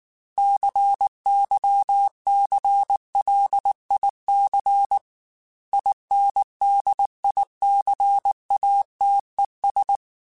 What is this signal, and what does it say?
Morse code, heard as a keyed tone, about 19 wpm, 800 Hz. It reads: CYCLIC INDICATES